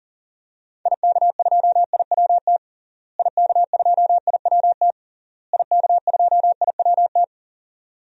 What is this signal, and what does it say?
Morse code, heard as a keyed tone, about 40 wpm, 700 Hz. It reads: IK2IWT IK2IWT IK2IWT
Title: DXCC-100 Call Signs - 6 Characters - Repeated 3x - 40wpm